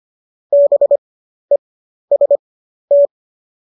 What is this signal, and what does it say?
Morse code, heard as a keyed tone, about 25 wpm, spaced as for 12 wpm, 570 Hz. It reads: BEST
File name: qso009 12